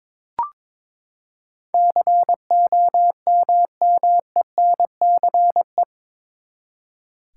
Morse code, heard as a keyed tone, about 22 wpm, 700 Hz. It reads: COMMENCE
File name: practice / 22wpm / Eight Letter Words - 22wpm